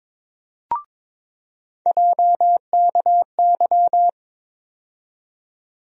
Morse code, heard as a keyed tone, about 22 wpm, 700 Hz. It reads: JKY
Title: USA Call Sign Postfixes - 3 Letters 22wpm